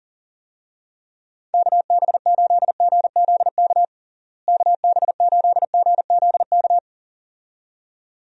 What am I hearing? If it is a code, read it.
Morse code, heard as a keyed tone, about 40 wpm, 700 Hz. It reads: KB8GZK KB8GZK
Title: USA Call Signs - 6 Characters - Repeated 2x - 40wpm